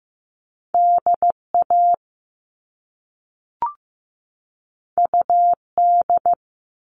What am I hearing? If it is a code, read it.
Morse code, heard as a keyed tone, about 15 wpm, 700 Hz. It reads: DA UD
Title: All Cut Numbers - 2-Digit 15wpm